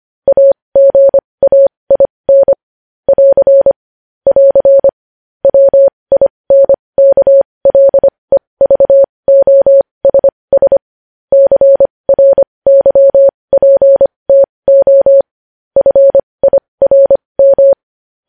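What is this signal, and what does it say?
Morse code, heard as a keyed tone, about 25 wpm, spaced as for 20 wpm, 550 Hz. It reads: AGAIN <AR> <AR> WINKLEVOSS CRYPTO FIRM
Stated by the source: Morse code practice sending news headlines at 20 wpm.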